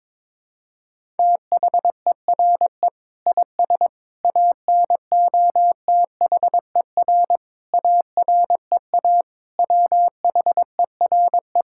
Morse code, heard as a keyed tone, about 22 wpm, 700 Hz. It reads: THERE IS ANOTHER AREA WHERE
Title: Sentences from Top 1000 Words - Review Part 5 - 22wpm